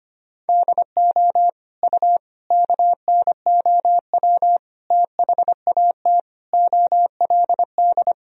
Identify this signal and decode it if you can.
Morse code, heard as a keyed tone, about 25 wpm, 700 Hz. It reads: DO U KNOW THAT OLD
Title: Sentences from Top 400 Words and Abbreviations 25wpm